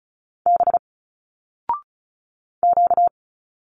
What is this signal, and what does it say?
Morse code, heard as a keyed tone, about 34 wpm, 700 Hz. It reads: B Q
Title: Single Letters 35wpm